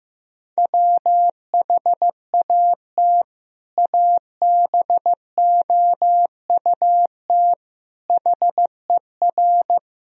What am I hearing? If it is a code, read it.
Morse code, heard as a keyed tone, about 15 wpm, 700 Hz. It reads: WHAT ABOUT HER
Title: Sentences from Top 1000 Words - Review Part 1 - 15wpm